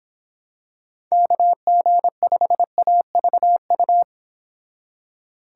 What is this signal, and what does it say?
Morse code, heard as a keyed tone, about 25 wpm, 700 Hz. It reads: KG5AVU
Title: Speed Racing - USA 2x3 Call Signs 22wpm